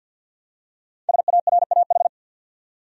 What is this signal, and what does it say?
Morse code, heard as a keyed tone, about 49 wpm, 700 Hz. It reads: SARAH
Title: USA Common Names - Max 5 Letters - 50wpm